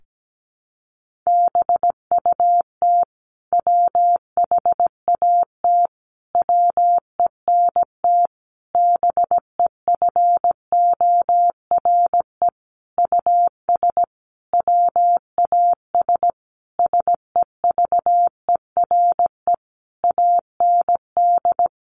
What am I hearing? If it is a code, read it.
Morse code, heard as a keyed tone, about 17 wpm, 700 Hz. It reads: BUT WHAT WENT BEFORE US WAS SEVERE AND
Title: Sentences from Top 1400 Words 17wpm